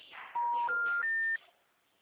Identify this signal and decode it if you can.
Special information tone.